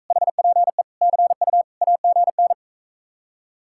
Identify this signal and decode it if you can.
Morse code, heard as a keyed tone, about 44 wpm, 700 Hz. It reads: HPE CU AGN